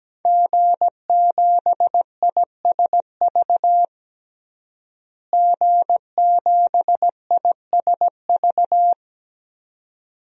Morse code, heard as a keyed tone, about 17 wpm, 700 Hz. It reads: G7ISV G7ISV
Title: DXCC-100 Call Signs - 5 Characters - Repeated 2x - 17wpm